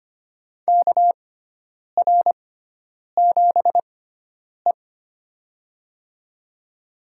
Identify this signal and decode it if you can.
Morse code, heard as a keyed tone, about 25 wpm, spaced as for 9 wpm, 700 Hz. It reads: KR7E
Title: USA Call Signs - 4 Characters - 6x Character Spacing - 25wpm